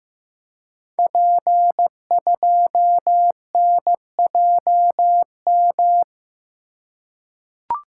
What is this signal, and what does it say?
Morse code, heard as a keyed tone, about 15 wpm, 700 Hz. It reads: P2NJM
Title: Code Groups - 5 Letter:Numbers - 15wpm